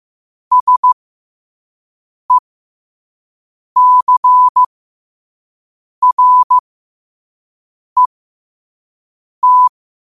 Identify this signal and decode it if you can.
Morse code, heard as a keyed tone, about 15 wpm, spaced as for 5 wpm, 1000 Hz. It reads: SECRET
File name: qotdfiles / 5wpm